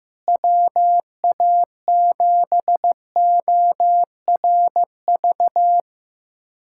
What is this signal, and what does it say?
Morse code, heard as a keyed tone, about 15 wpm, 700 Hz. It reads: WA7ORV
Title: USA Callsign DE Callsign - 6 Characters - 15wpm